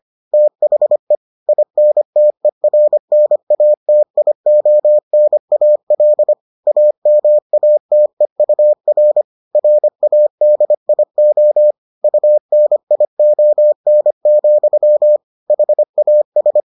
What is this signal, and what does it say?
Morse code, heard as a keyed tone, about 25 wpm, 600 Hz. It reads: THE INTERNATIONAL AMATEUR RADIO UNION, HAS